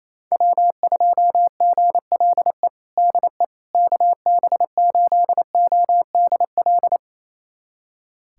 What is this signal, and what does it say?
Morse code, heard as a keyed tone, about 28 wpm, 700 Hz. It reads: W2GLE DE KB8ODL
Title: USA Callsign DE Callsign 28wpm